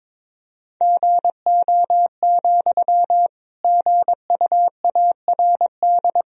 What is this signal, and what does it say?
Morse code, heard as a keyed tone, about 22 wpm, 700 Hz. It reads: GO, GUARD
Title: Sentences from Top 1000 Words - Review Part 4 - 22wpm